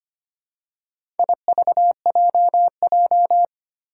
Morse code, heard as a keyed tone, about 25 wpm, 700 Hz. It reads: IVJJ
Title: Four Challenging Letter-Number Combinations 25wpm